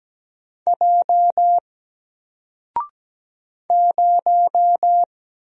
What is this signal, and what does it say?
Morse code, heard as a keyed tone, about 17 wpm, 700 Hz. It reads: J 0
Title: Single Character (Letter:Number:Punctuation:Prosign) 17wpm